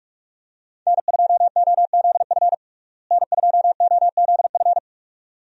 Morse code, heard as a keyed tone, about 45 wpm, 700 Hz. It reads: N2OZF N2OZF
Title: USA Call Signs - 5 Characters - Repeated 2x - 45wpm